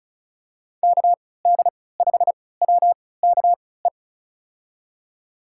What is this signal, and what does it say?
Morse code, heard as a keyed tone, about 34 wpm, spaced as for 20 wpm, 700 Hz. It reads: KD5WKE
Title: USA Call Signs - 6 Characters - 3x Character Spacing - 35wpm